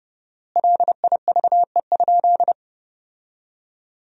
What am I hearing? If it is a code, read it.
Morse code, heard as a keyed tone, about 30 wpm, 700 Hz. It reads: LIVE?
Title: 4x Word-Spacing - Sentences from Top 200 Words - 30wpm